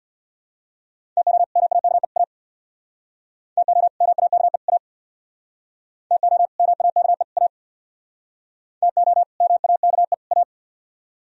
Morse code, heard as a keyed tone, about 67 wpm, 700 Hz. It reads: TO MAKE A TO MAKE A TO MAKE A TO MAKE A